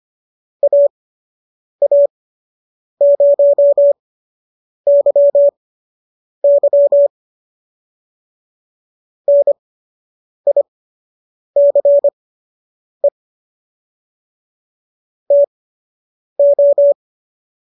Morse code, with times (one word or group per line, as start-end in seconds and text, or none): 0.6-7.1 AA0YY
9.3-13.1 NICE
15.3-16.9 TO